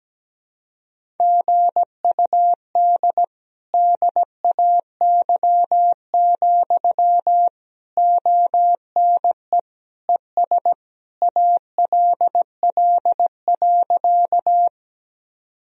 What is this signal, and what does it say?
Morse code, heard as a keyed tone, about 17 wpm, 700 Hz. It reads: GUD DAY, ONE ES ALL.
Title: Sentences from Top 200 Words and Abbreviations 17wpm